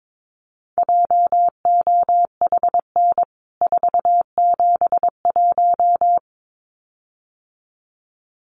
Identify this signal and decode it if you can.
Morse code, heard as a keyed tone, about 22 wpm, 700 Hz. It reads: JOHN 471
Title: CWT Exchange 22wpm